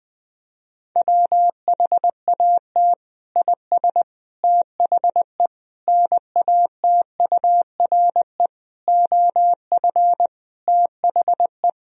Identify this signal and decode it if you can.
Morse code, heard as a keyed tone, about 21 wpm, 700 Hz. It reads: WHAT IS THE NATURE OF THE
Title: Sentences from Top 1000 Words - Review Part 3 - 20wpm